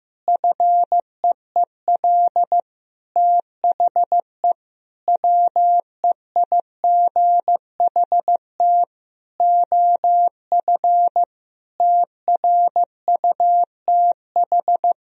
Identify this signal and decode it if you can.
Morse code, heard as a keyed tone, about 15 wpm, 700 Hz. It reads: FEEL THE WEIGHT OF TRUTH
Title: Sentences from Top 1000 Words - Review Part 3 - 15wpm